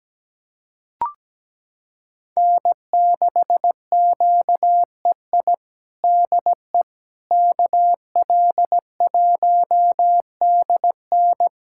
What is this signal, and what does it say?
Morse code, heard as a keyed tone, about 17 wpm, 700 Hz. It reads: N6QEI DE KL1DN
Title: USA Callsign DE Callsign - 5 Characters - 17wpm